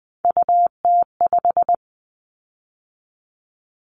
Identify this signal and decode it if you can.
Morse code, heard as a keyed tone, about 20 wpm, 700 Hz. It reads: UT5
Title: Callsign Prefixes for Top 25 DXCC - 20wpm